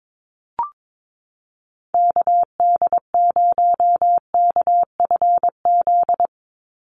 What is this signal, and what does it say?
Morse code, heard as a keyed tone, about 22 wpm, 700 Hz. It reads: KD0KFZ
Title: USA Call Signs 22wpm